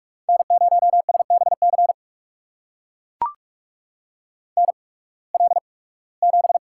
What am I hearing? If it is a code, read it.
Morse code, heard as a keyed tone, about 44 wpm, spaced as for 23 wpm, 700 Hz. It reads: N0SBC NL7
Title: USA Call Signs - 5 Characters - 8x Character Spacing - 45wpm